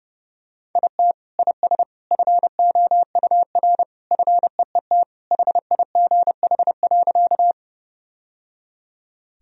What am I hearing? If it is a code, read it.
Morse code, heard as a keyed tone, about 30 wpm, 700 Hz. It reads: IT IS FOUR FEET HIGH.
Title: Sentences from Top 1000 Words - Review Part 1 - 30wpm